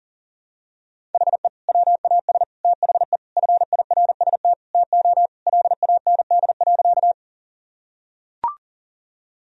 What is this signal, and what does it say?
Morse code, heard as a keyed tone, about 40 wpm, 700 Hz. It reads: HE WAS THE FIRST TO LAND.
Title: Sentences from Top 1000 Words - Review Part 1 - 40wpm